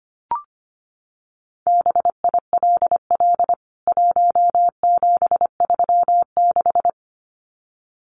Morse code, heard as a keyed tone, about 25 wpm, 700 Hz. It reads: BILL 1736
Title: CWT Exchange 25wpm